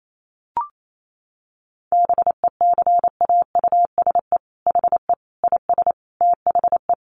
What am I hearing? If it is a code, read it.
Morse code, heard as a keyed tone, about 28 wpm, 700 Hz. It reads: BECAUSE HE IS THE
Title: Sentences from Top 1500 Words 28wpm